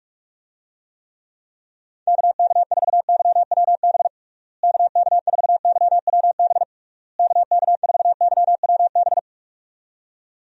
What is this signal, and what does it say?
Morse code, heard as a keyed tone, about 46 wpm, 700 Hz. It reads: KK4YWB KK4YWB KK4YWB
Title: USA Call Signs - 6 Characters - Repeated 3x - 45wpm